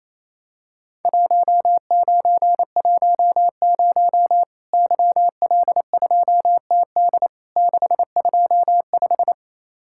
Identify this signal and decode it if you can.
Morse code, heard as a keyed tone, about 28 wpm, 700 Hz. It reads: 1910 YL2TD 625
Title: Worked All Europe DX Contest - QTC Exchange - 28wpm